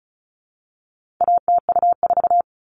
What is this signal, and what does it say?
Morse code, heard as a keyed tone, about 34 wpm, 700 Hz. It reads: ATU4
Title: All Cut Numbers - 4-Digit 35wpm